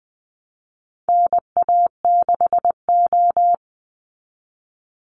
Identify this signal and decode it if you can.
Morse code, heard as a keyed tone, about 20 wpm, 700 Hz. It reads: NA6O
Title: Call Signs - 20wpm